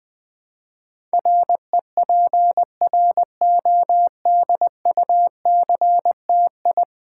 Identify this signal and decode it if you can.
Morse code, heard as a keyed tone, about 20 wpm, 700 Hz. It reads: REPRODUCTI
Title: Build A Word - 5 Letters At A Time - 12-Letter Words - 20wpm